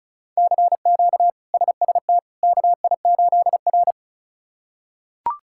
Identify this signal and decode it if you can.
Morse code, heard as a keyed tone, about 34 wpm, 700 Hz. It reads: CQ SST KI8R
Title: K1USN SST QSO Practice - 35wpm